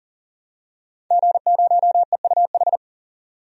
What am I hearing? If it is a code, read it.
Morse code, heard as a keyed tone, about 40 wpm, 700 Hz. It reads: G0EUH
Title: DXCC-100 Call Signs - 5 Characters - 40wpm